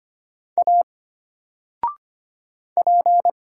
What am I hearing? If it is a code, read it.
Morse code, heard as a keyed tone, about 25 wpm, 700 Hz. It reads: A P
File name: Single-Letter-Number-25wpm